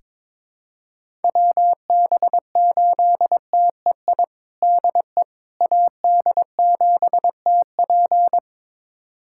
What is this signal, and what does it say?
Morse code, heard as a keyed tone, about 22 wpm, 700 Hz. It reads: WB8TEI DE AD7TP
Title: USA Callsign DE Callsign 22wpm